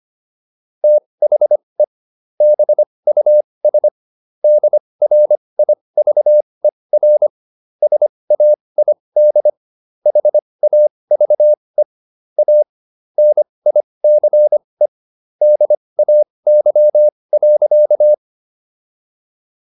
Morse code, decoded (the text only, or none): THE BUS DRIVER SAID HAVE A NICE DAY.